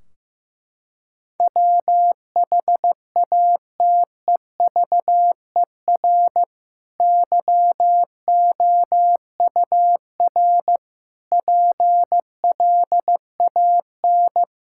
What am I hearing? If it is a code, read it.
Morse code, heard as a keyed tone, about 15 wpm, 700 Hz. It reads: WHATEVER YOUR PLAN
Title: Sentences from Top 1600 Words 15wpm